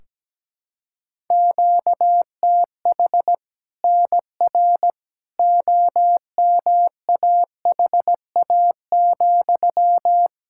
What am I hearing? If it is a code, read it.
Morse code, heard as a keyed tone, about 17 wpm, 700 Hz. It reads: QTH NR OMAHA,
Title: Practice QSO - Exchange 1 - Beginning - 17wpm